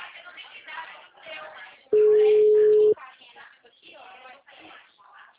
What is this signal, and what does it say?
Ringback tone (Europe, 425 Hz).